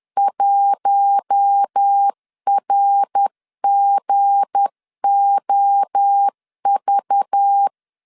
Morse code, heard as a keyed tone, about 10 wpm, 800 Hz. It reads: 1RGOV
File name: LEZIONI / 7 GF9Z / 7 G 10wpm 2501